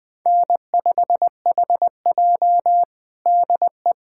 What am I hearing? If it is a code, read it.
Morse code, heard as a keyed tone, about 20 wpm, 700 Hz. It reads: N5HJ DE
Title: USA Callsign DE Callsign - 4 Characters - 20wpm